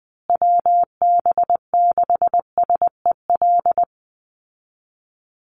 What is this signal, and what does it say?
Morse code, heard as a keyed tone, about 20 wpm, 700 Hz. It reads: WB6SEL
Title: Call Signs Repeated 3x - 20wpm